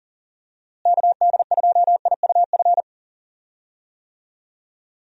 Morse code, heard as a keyed tone, about 40 wpm, 700 Hz. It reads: KD2IUF
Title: USA Call Signs - 6 Characters - Repeated 2x - 40wpm